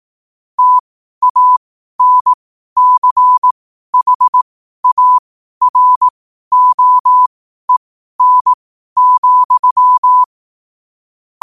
Morse code, heard as a keyed tone, about 18 wpm, spaced as for 12 wpm, 1000 Hz. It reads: TANCHAROEN,
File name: qotdfiles / 12wpm